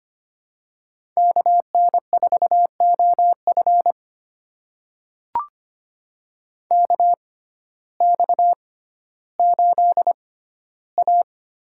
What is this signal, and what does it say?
Morse code, heard as a keyed tone, about 25 wpm, spaced as for 14 wpm, 700 Hz. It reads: KN4OF KX8A
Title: USA Call Signs - 6x Character Spacing - 25wpm